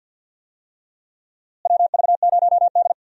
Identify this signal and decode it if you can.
Morse code, heard as a keyed tone, about 52 wpm, 700 Hz. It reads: WV0D
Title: USA Call Signs - 4 Characters - Repeated 2x - 50wpm